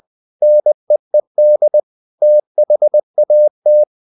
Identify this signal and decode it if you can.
Morse code, heard as a keyed tone, about 20 wpm, 600 Hz. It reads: NEED THAT
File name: the old man and the sea part 2 20wpm